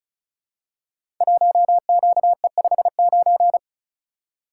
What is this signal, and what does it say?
Morse code, heard as a keyed tone, about 36 wpm, 700 Hz. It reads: 1QE59
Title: Code Groups - 5 Letter:Numbers - 35wpm